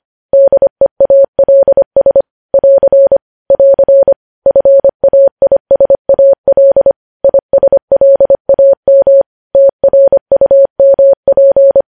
Morse code, heard as a keyed tone, about 25 wpm, 550 Hz. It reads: DEALS <AR> <AR> FAISAL ISLAM TRUMP
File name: podcast / News headlines-1x-25x25